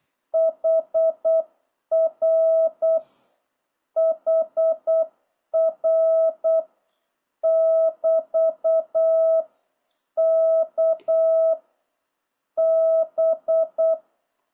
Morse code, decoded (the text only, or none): HR HR <BT> K B